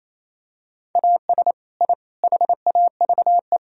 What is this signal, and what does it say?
Morse code, heard as a keyed tone, about 28 wpm, 700 Hz. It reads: AS I HAVE
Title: Sentences from Top 1000 Words - Review Part 3 - 28wpm